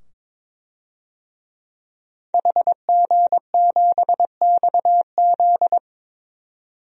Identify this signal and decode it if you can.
Morse code, heard as a keyed tone, about 22 wpm, 700 Hz. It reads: HG7XZ
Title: DXCC-100 Call Signs - 5 Characters - 22wpm